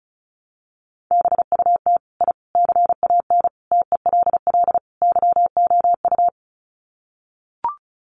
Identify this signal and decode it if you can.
Morse code, heard as a keyed tone, about 34 wpm, 700 Hz. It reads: BUT I CAN TELL YOU
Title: Sets of 5 Words Encore 35wpm